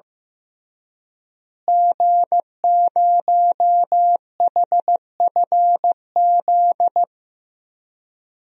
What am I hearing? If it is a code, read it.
Morse code, heard as a keyed tone, about 16 wpm, 700 Hz. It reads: G0HFZ
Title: DXCC-100 Call Signs - 5 Characters - Repeated 3x - 15wpm